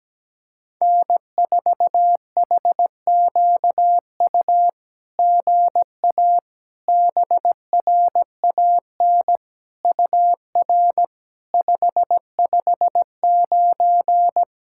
Morse code, heard as a keyed tone, about 17 wpm, 700 Hz. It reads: N4HQU GA BRAN UR 559
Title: SOTA QSO Practice 17wpm